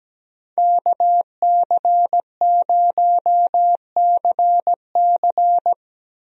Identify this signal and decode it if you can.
Morse code, heard as a keyed tone, about 17 wpm, 700 Hz. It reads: KC0CC
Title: USA Call Signs - 5 Characters - 17wpm